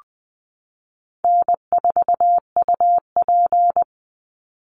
Morse code, heard as a keyed tone, about 20 wpm, 700 Hz. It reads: N4UP
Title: Call Signs Repeated 2x - 20wpm